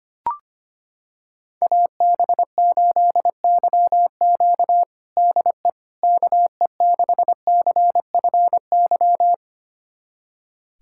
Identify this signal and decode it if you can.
Morse code, heard as a keyed tone, about 25 wpm, 700 Hz. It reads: AB8YQ DE KE6CFY
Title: USA Callsign DE Callsign 25wpm